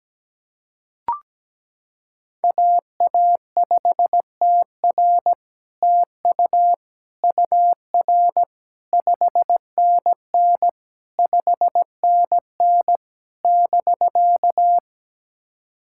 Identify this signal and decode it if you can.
Morse code, heard as a keyed tone, about 17 wpm, 700 Hz. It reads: AA5TR TU UR 5NN 5NN <BK>
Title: SOTA QSO Practice 17wpm